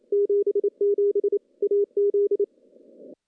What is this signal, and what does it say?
Morse code, heard as a keyed tone, about 28 wpm, 410 Hz. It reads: 77 AZ
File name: CW-SS-k5tr